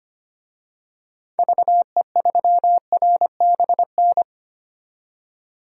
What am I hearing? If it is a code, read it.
Morse code, heard as a keyed tone, about 25 wpm, 700 Hz. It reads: VE3RBN
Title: DXCC-100 Call Signs - 6 Characters - 25wpm